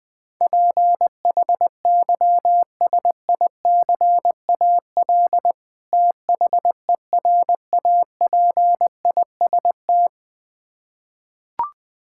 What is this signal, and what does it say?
Morse code, heard as a keyed tone, about 20 wpm, 700 Hz. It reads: PHYSICAL THERAPIST
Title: Practice QSO - Exchange 3 - More Info - 20wpm